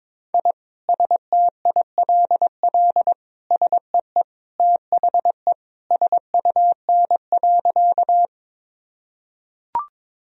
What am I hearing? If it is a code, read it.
Morse code, heard as a keyed tone, about 22 wpm, 700 Hz. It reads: I STILL SEE THE SUN.
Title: Sentences from Top 1000 Words - Review Part 1 - 22wpm